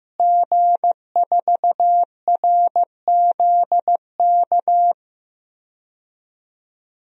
Morse code, heard as a keyed tone, about 15 wpm, 700 Hz. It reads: G4RZK
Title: DXCC-100 Call Signs - 5 Characters - Repeated 3x - 15wpm